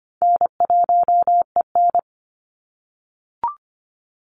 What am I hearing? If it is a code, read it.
Morse code, heard as a keyed tone, about 25 wpm, 700 Hz. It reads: N1EN
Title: Call Signs - 25wpm